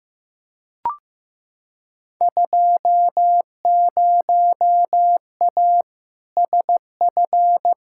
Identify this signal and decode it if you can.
Morse code, heard as a keyed tone, about 15 wpm, 700 Hz. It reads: 20A SF
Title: Field Day Exchanges Encore - 15wpm